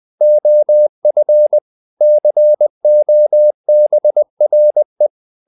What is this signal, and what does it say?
Morse code, heard as a keyed tone, about 20 wpm, 600 Hz. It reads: OF COBRE
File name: the old man and the sea / the old man and the sea part 1 20wpm